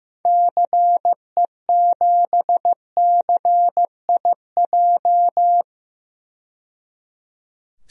Morse code, heard as a keyed tone, about 15 wpm, 700 Hz. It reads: CE7CIJ
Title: DXCC-100 Call Signs - 6 Characters - 15wpm